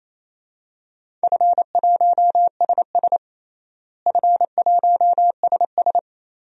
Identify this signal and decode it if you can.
Morse code, heard as a keyed tone, about 28 wpm, 700 Hz. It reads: F1SS F1SS
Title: DXCC-100 Call Signs - 4 Characters - Repeated 2x - 28wpm